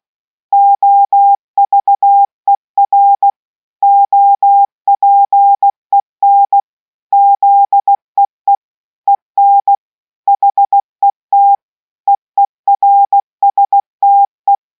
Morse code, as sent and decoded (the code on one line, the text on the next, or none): --- ...- . .-. / --- .--. -. / --.. . / . -. / .... - / . .-. ... - .
OVER OPEN ZEE EN HET EERSTE